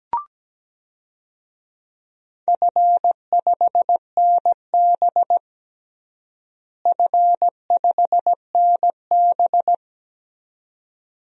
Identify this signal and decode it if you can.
Morse code, heard as a keyed tone, about 17 wpm, 700 Hz. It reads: F5NB F5NB
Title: DXCC-100 Call Signs - 4 Characters - Repeated 2x - 17wpm